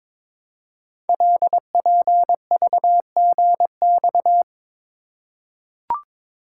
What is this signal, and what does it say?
Morse code, heard as a keyed tone, about 22 wpm, 700 Hz. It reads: LPVGX
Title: Code Groups - 5 Letters - 22wpm